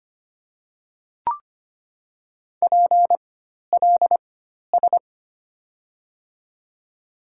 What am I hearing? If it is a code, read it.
Morse code, heard as a keyed tone, about 25 wpm, spaced as for 12 wpm, 700 Hz. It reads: PLS
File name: Top 90 QSO Elements - 4x Character Spacing - 25wpm